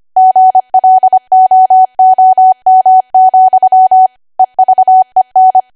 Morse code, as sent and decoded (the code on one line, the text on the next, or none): --. .-.. --- --- -- --..-- / . ...- . -.
GLOOM, EVEN